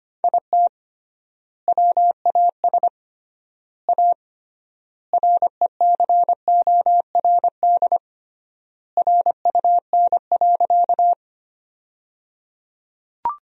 Morse code, heard as a keyed tone, about 25 wpm, 700 Hz. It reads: IT WAS A RECORD RUN.
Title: 3x Word-Spacing - Sentences from Top 500 Words - 25wpm